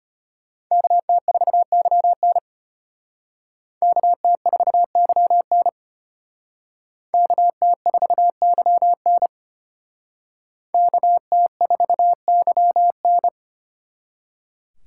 Morse code, heard as a keyed tone, about 32 wpm, 700 Hz. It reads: KT4YN KT4YN KT4YN KT4YN